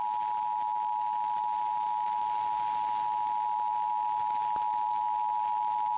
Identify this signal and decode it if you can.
Emergency Alert System (SAME) tones (the 853 and 960 Hz attention tone). It is heard from 0.2 to 5.6 s.
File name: eas / KKPRKXPN / KKPRKXPN